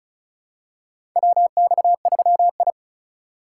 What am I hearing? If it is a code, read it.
Morse code, heard as a keyed tone, about 36 wpm, 700 Hz. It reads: WX3I
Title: USA Call Signs - 4 Characters - 8x Character Spacing - 35wpm